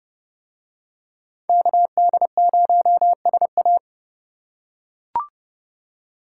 Morse code, heard as a keyed tone, about 30 wpm, 700 Hz. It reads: KD0SA